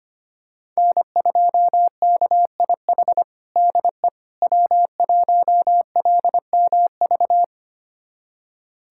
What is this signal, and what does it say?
Morse code, heard as a keyed tone, about 25 wpm, 700 Hz. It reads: N2KIH DE W1LMV
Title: USA Callsign DE Callsign - 5 Characters - 25wpm